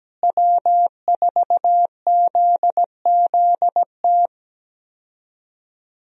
Morse code, heard as a keyed tone, about 17 wpm, 700 Hz. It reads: W4ZZT
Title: USA Call Signs - 5 Characters - Repeated 3x - 17wpm